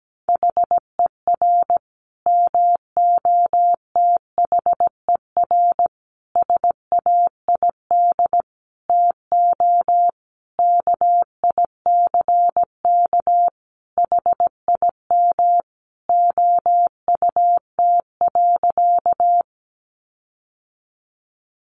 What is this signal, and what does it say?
Morse code, heard as a keyed tone, about 17 wpm, 700 Hz. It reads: HER MOTHER SAID TO KICK HIM OUT.
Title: Sentences from Top 1300 Words 17wpm